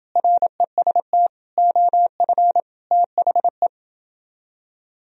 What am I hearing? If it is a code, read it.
Morse code, heard as a keyed tone, about 29 wpm, 700 Hz. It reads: REST OF THE